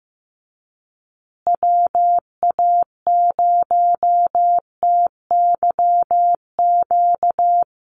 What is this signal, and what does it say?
Morse code, heard as a keyed tone, about 15 wpm, 700 Hz. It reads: WA0TYQ
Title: USA Call Signs Repeated 2x 15wpm